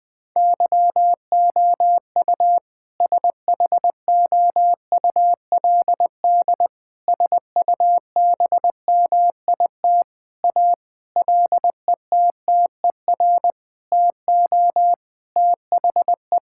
Morse code, heard as a keyed tone, about 21 wpm, 700 Hz. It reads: YOU SHOULD SUBMIT A LETTER TO THE
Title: Sentences with 6-Word Phrases - Spoken Beginning - 20wpm